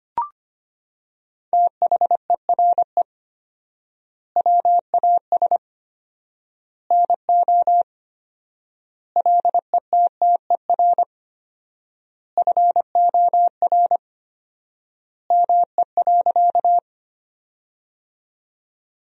Morse code, heard as a keyed tone, about 25 wpm, 700 Hz. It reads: THERE WAS NO LETTER FOR ME.
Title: 4x Word-Spacing - Sentences from Top 300 Words - 25wpm